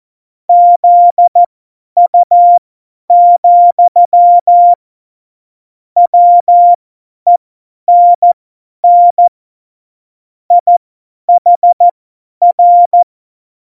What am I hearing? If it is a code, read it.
Morse code, heard as a keyed tone, about 14 wpm, spaced as for 10 wpm, 700 Hz. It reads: ZU, WENN IHR
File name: Morsekurs / Witze / Witz 03. Oktober 17, Tempo 50 F70